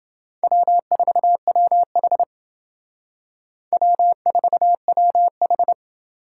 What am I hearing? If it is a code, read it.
Morse code, heard as a keyed tone, about 29 wpm, 700 Hz. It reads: W4WH W4WH